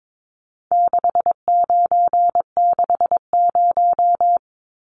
Morse code, heard as a keyed tone, about 22 wpm, 700 Hz. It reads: 6960
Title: Numbers - 4-Digit 22wpm